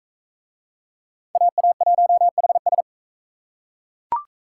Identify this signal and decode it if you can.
Morse code, heard as a keyed tone, about 40 wpm, 700 Hz. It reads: AA1HS